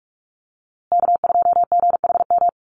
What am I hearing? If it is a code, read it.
Morse code, heard as a keyed tone, about 44 wpm, 700 Hz. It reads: K2GHM
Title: USA Call Signs Repeated 3x 45wpm